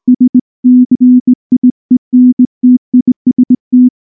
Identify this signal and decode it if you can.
Morse code, heard as a keyed tone, about 19 wpm, 250 Hz. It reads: SCIENTIST
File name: morse / morse-training-20WPM-7